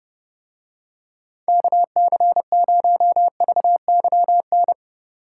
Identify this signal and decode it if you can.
Morse code, heard as a keyed tone, about 29 wpm, 700 Hz. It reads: KC0VYN